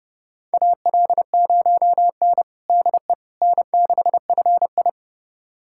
Morse code, heard as a keyed tone, about 30 wpm, 700 Hz. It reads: AL0N DE N6FI
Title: USA Callsign DE Callsign - 4 Characters - 30wpm